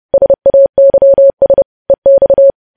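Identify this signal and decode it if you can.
Morse code, heard as a keyed tone, about 30 wpm, 550 Hz. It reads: SAYS EX
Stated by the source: News headlines sent in Morse code at 30 wpm.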